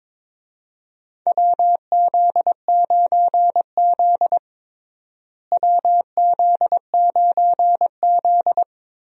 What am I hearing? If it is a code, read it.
Morse code, heard as a keyed tone, about 22 wpm, 700 Hz. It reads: WZ9Z WZ9Z
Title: USA Call Signs - 4 Characters - Repeated 3x - 22wpm